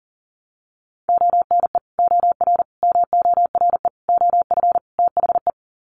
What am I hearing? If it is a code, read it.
Morse code, heard as a keyed tone, about 40 wpm, 700 Hz. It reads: ONE OR MORE OF THE
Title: Sets of 5 Words Encore 40wpm